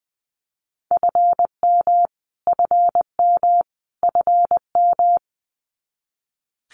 Morse code, heard as a keyed tone, about 20 wpm, 700 Hz. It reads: FM FM FM
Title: Abbreviations Repeated 3x - 20wpm